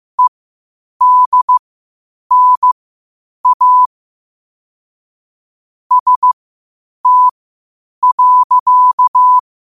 Morse code, heard as a keyed tone, about 15 wpm, spaced as for 8 wpm, 1000 Hz. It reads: EDNA ST.
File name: qotdfiles / 8wpm